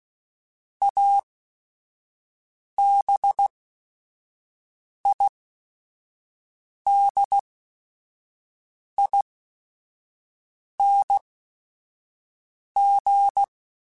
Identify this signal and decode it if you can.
Morse code, heard as a keyed tone, about 16 wpm, spaced as for 5 wpm, 800 Hz. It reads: ABIDING
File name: Special INGs / Special INGs-mp3s / Sngl Ing @5